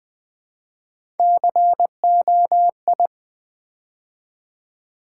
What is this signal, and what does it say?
Morse code, heard as a keyed tone, about 20 wpm, 700 Hz. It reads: COI
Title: Build A Word - 3 Letters At A Time - 8-Letter Words - 20wpm